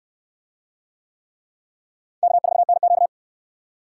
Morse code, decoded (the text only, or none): K4NQ